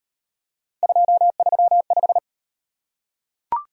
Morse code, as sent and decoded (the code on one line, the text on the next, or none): ..--- ...-- .....
235